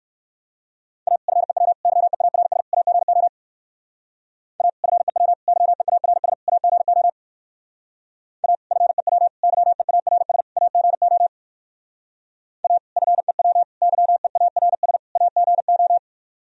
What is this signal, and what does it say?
Morse code, heard as a keyed tone, about 58 wpm, 700 Hz. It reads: A FEW YEARS AGO A FEW YEARS AGO A FEW YEARS AGO A FEW YEARS AGO